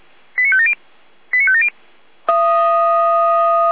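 Five-tone selective call (ZVEI), heard as quick runs of five tones, tones of about 70 ms. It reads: 89570 89570
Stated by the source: Five-tone ZVEI selective call received by SDR, from audio.f85360.feuerwehr-orlamuende.org, the Orlamünde fire brigade.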